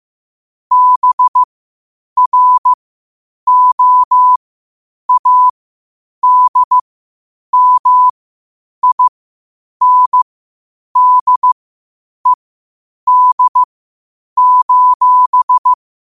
Morse code, heard as a keyed tone, about 15 wpm, spaced as for 8 wpm, 1000 Hz. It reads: BROADMINDED:
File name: qotdfiles / 8wpm